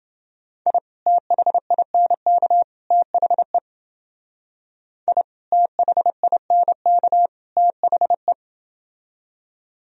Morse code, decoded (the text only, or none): I THINK THE I THINK THE